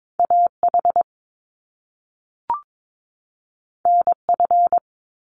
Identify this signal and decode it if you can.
Morse code, heard as a keyed tone, about 22 wpm, 700 Hz. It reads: AH NF
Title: Top Two Letter Combinations in Words 22wpm